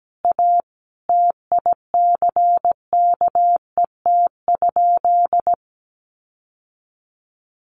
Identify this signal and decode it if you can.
Morse code, heard as a keyed tone, about 17 wpm, 700 Hz. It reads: A TICKET?
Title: Sentences from Top 1400 Words 17wpm